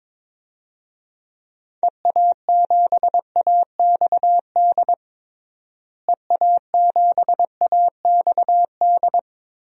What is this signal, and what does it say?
Morse code, heard as a keyed tone, about 22 wpm, 700 Hz. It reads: EA7AXD EA7AXD
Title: DXCC-100 Call Signs - 6 Characters - Repeated 3x - 22wpm